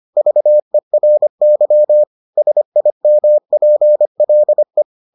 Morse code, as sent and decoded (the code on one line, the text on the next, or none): ...- . .-. -.-- / ... .. -- .--. .-.. .
VERY SIMPLE